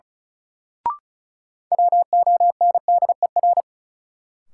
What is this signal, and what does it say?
Morse code, heard as a keyed tone, about 36 wpm, 700 Hz. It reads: WONDER
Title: Sentences from Top 500 Words - New Word Drill - 35wpm